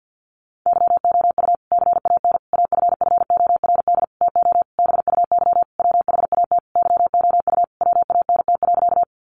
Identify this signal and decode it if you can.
Morse code, heard as a keyed tone, about 50 wpm, 700 Hz. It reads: YOU CAN AFFORD TO BUY WHAT YOU WANT.
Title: Sentences from Top 1100 Words 50wpm